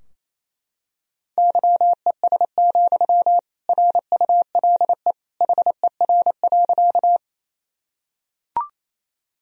Morse code, heard as a keyed tone, about 28 wpm, 700 Hz. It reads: YES, RULE HER.
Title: Sentences from Top 1000 Words - Review Part 2 - 28wpm